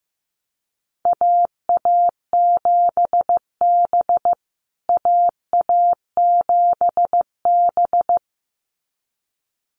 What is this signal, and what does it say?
Morse code, heard as a keyed tone, about 15 wpm, 700 Hz. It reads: AA7B AA7B
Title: USA Call Signs Repeated 2x 15wpm